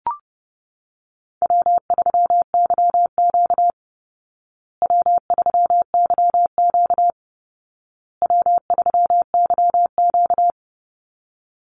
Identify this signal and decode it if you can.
Morse code, heard as a keyed tone, about 30 wpm, 700 Hz. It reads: W3YQ W3YQ W3YQ
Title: Call Signs Repeated 2x - 30wpm